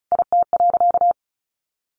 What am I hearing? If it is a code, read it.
Morse code, heard as a keyed tone, about 34 wpm, 700 Hz. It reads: IT.
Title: A Sound Of Thunder 35wpm